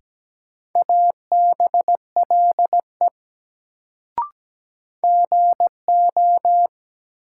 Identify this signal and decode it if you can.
Morse code, heard as a keyed tone, about 17 wpm, 700 Hz. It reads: ABLE GO
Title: Sets of 2 Words - New Word Drill - 17wpm